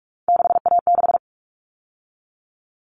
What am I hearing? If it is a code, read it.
Morse code, heard as a keyed tone, about 44 wpm, 700 Hz. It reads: 6A6
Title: All Cut Numbers - 3-Digit 45wpm